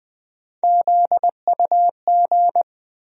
Morse code, heard as a keyed tone, about 20 wpm, 700 Hz. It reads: ZUG
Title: USA Call Sign Postfixes - 3 Letters 20wpm